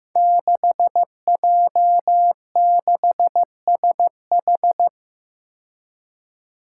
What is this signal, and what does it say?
Morse code, heard as a keyed tone, about 15 wpm, 700 Hz. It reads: 6J6SH
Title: Five Challenging Letter-Number Combinations 15wpm